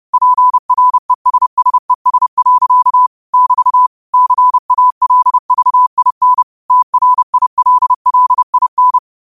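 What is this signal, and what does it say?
Morse code, heard as a keyed tone, about 30 wpm, 1000 Hz. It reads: PRESSES. <BT> CALVIN TRILLIN
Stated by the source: Morse quote of the day at 30 WPM.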